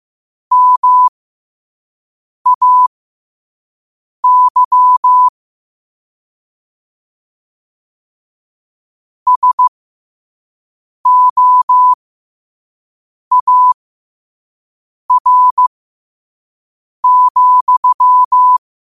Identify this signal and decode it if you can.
Morse code, heard as a keyed tone, about 15 wpm, spaced as for 5 wpm, 1000 Hz. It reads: MAY SOAR,
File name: qotdfiles / 5wpm